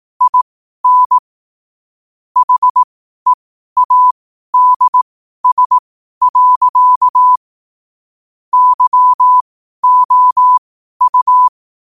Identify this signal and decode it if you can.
Morse code, heard as a keyed tone, about 18 wpm, spaced as for 12 wpm, 1000 Hz. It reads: IN HEADS. YOU
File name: qotdfiles / 12wpm